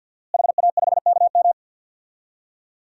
Morse code, heard as a keyed tone, about 47 wpm, 700 Hz. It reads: HA5KM
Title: DXCC-100 Call Signs - 5 Characters - Repeated 2x - 50wpm